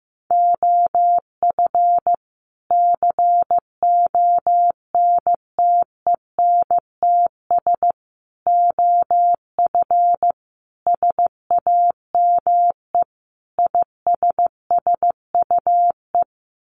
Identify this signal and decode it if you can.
Morse code, heard as a keyed tone, about 15 wpm, 700 Hz. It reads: OF CONTENTS OF SAME ISSUE
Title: Sets of 5 Words Encore2 15wpm